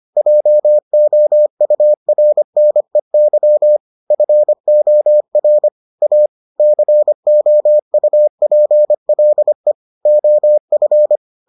Morse code, heard as a keyed tone, about 25 wpm, 600 Hz. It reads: JOURNEY FOR A COUPLE OF